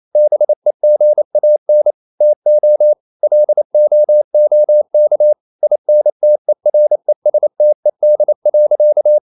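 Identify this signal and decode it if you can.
Morse code, heard as a keyed tone, about 28 wpm, 600 Hz. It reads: BEGAN TO LOOK INTERESTED.
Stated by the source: Morse code practice at 28 wpm, text of Ken Follett's novel Never in English, from CW med Gnister.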